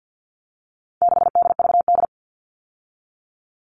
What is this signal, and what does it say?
Morse code, heard as a keyed tone, about 50 wpm, 700 Hz. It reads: 6DVD
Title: All Cut Numbers - 4-Digit 50wpm